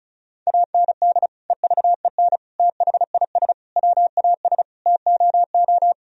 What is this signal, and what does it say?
Morse code, heard as a keyed tone, about 36 wpm, 700 Hz. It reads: AND EVEN THIS WAS TOO
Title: Sentences from Top 1000 Words - Review Part 1 - 35wpm